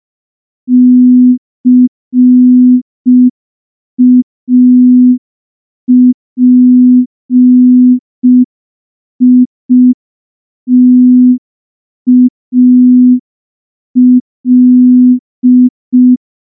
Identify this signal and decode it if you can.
Morse code, heard as a keyed tone, about 5 wpm, 250 Hz. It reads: CAPITAL
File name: morse / morse-training-05WPM